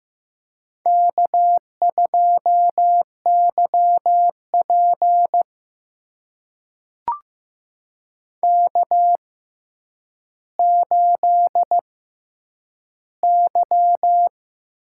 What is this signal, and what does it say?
Morse code, heard as a keyed tone, about 15 wpm, spaced as for 9 wpm, 700 Hz. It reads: K2YP K8Y
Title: USA Call Signs - 4 Characters - 6x Character Spacing - 15wpm